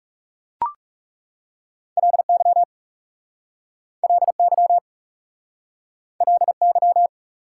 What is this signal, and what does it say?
Morse code, heard as a keyed tone, about 40 wpm, 690 Hz. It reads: LY LY LY